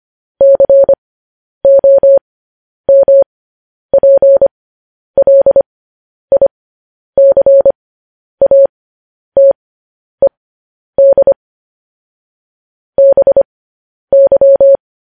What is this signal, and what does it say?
Morse code, heard as a keyed tone, about 25 wpm, spaced as for 10 wpm, 550 Hz. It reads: COMPLICATED BY
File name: podcast / News headlines-1x-25x10